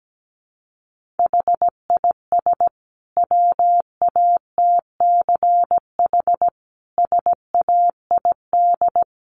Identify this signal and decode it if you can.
Morse code, heard as a keyed tone, about 17 wpm, 700 Hz. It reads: HIS WATCH SAID
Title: Sentences from Top 1300 Words 17wpm